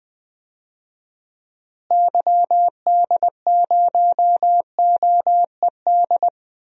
Morse code, heard as a keyed tone, about 20 wpm, 700 Hz. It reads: YD0OED
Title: DXCC-100 Call Signs - 6 Characters - Repeated 3x - 20wpm